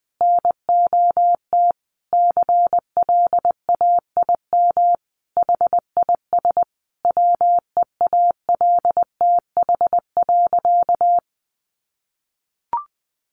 Morse code, heard as a keyed tone, about 20 wpm, 700 Hz. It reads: NOT CLAIM HIS WEALTH.
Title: Sentences from Top 1500 Words 20wpm